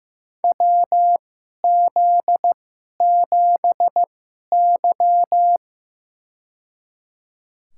Morse code, heard as a keyed tone, about 15 wpm, spaced as for 11 wpm, 700 Hz. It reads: WZ7Y
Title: USA Call Signs - 4 Characters - 2x Character Spacing - 15wpm